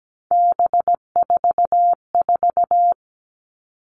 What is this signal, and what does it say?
Morse code, heard as a keyed tone, about 17 wpm, 700 Hz. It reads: B44
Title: All Cut Numbers - 3-Digit 17wpm